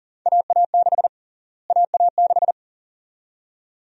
Morse code, heard as a keyed tone, about 40 wpm, 700 Hz. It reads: AA6 AA6
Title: USA Call Sign Prefixes - Repeated 2x - 40wpm